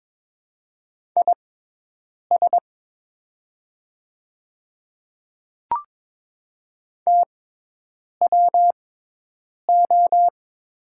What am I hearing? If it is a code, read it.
Morse code, heard as a keyed tone, about 22 wpm, spaced as for 8 wpm, 700 Hz. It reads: IS TWO
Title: 6x Character Spacing - Top 200 Words - 22wpm